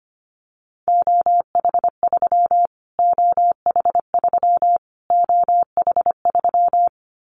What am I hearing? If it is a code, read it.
Morse code, heard as a keyed tone, about 25 wpm, 700 Hz. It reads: OH3 OH3 OH3
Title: Callsign Prefixes for Top 25 DXCC - Repeated 3x - 25wpm